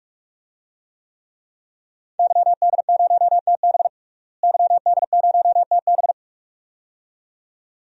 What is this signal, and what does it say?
Morse code, heard as a keyed tone, about 46 wpm, 700 Hz. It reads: YD0TB YD0TB
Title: DXCC-100 Call Signs - 5 Characters - Repeated 2x - 45wpm